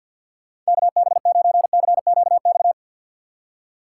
Morse code, heard as a keyed tone, about 66 wpm, spaced as for 58 wpm, 700 Hz. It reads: KB9CQX
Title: Speed Racing - USA 2x3 Call Signs 50wpm